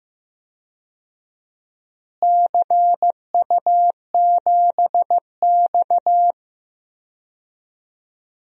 Morse code, heard as a keyed tone, about 15 wpm, 700 Hz. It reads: CU7X
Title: DXCC-100 Call Signs - 4 Characters - Repeated 2x - 15wpm